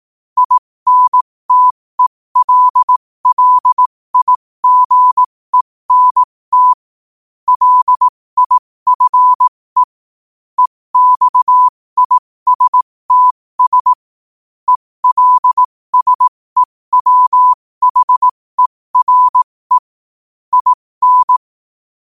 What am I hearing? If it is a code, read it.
Morse code, heard as a keyed tone, about 18 wpm, spaced as for 15 wpm, 1000 Hz. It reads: INTELLIGENT LIFE EXISTS ELSEWHERE IN
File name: qotdfiles / 15wpm